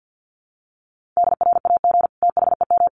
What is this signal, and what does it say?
Morse code, heard as a keyed tone, about 50 wpm, 700 Hz. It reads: DRAG THEM
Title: Sentences from Top 1500 Words 50wpm